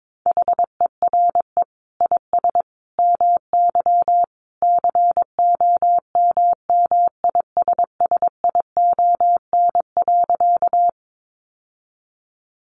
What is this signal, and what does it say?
Morse code, heard as a keyed tone, about 22 wpm, 700 Hz. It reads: HERE IS MY COMMISSION.
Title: Sentences from Top 1400 Words 22wpm